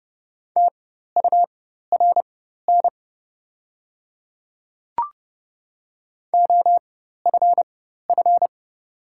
Morse code, heard as a keyed tone, about 30 wpm, spaced as for 14 wpm, 700 Hz. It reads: TURN OFF
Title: 4x Character Spacing - Top 300 Words - 30wpm